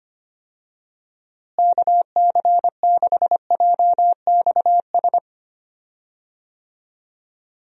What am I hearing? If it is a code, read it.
Morse code, heard as a keyed tone, about 25 wpm, 700 Hz. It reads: KC6JXS
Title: USA Call Signs - 6 Characters - Repeated 2x - 25wpm